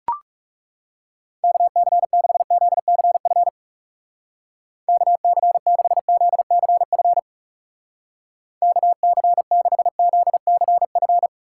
Morse code, heard as a keyed tone, about 38 wpm, 700 Hz. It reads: KC6ZCF KC6ZCF KC6ZCF